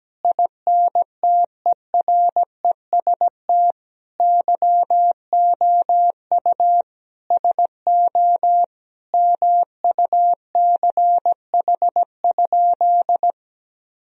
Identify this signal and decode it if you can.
Morse code, heard as a keyed tone, about 17 wpm, 700 Hz. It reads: INTEREST YOU SO MUCH?
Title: Sentences from Top 1000 Words - Review Part 2 - 17wpm